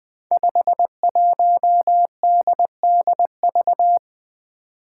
Morse code, heard as a keyed tone, about 20 wpm, 700 Hz. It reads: 51DDV
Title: Five Challenging Letter-Number Combinations 20wpm